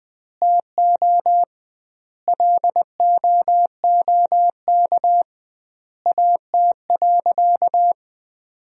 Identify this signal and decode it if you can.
Morse code, heard as a keyed tone, about 20 wpm, 700 Hz. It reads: TO LOOK AT.
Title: 2x Word-Spacing - Sentences from Top 200 Words - 20wpm